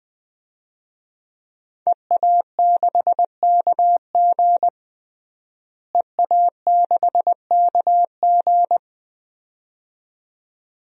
Morse code, heard as a keyed tone, about 21 wpm, 700 Hz. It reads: EA6KG EA6KG
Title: DXCC-100 Call Signs - 5 Characters - Repeated 2x - 20wpm